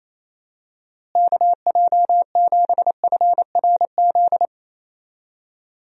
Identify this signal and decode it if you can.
Morse code, heard as a keyed tone, about 28 wpm, 700 Hz. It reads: KJ7FRZ
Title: USA Call Signs - 6 Characters - Repeated 3x - 28wpm